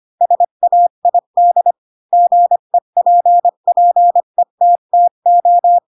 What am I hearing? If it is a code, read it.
Morse code, heard as a keyed tone, about 25 wpm, 700 Hz. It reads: SAID GEPPETTO
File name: Pinocchio / Pinocchio ch3 20 25 ik1hge